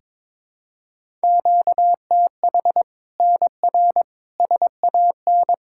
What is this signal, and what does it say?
Morse code, heard as a keyed tone, about 22 wpm, 700 Hz. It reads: QTH NR SAN
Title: Practice QSO - Exchange 1 - Beginning - 22wpm